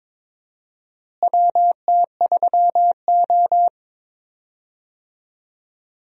Morse code, heard as a keyed tone, about 21 wpm, 700 Hz. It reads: WT3O
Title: USA Call Signs - 4 Characters - Repeated 2x - 22wpm